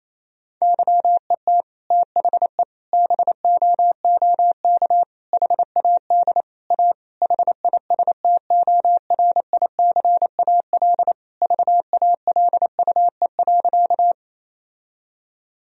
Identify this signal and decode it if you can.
Morse code, heard as a keyed tone, about 28 wpm, 700 Hz. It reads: YET THE BOOK HAD A HISTORICAL VALUE.
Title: Sentences from Top 1000 Words - Review Part 4 - 28wpm